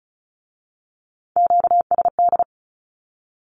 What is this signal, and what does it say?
Morse code, heard as a keyed tone, about 36 wpm, 700 Hz. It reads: QSD
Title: Q-Codes with Spoken Meaning - 35wpm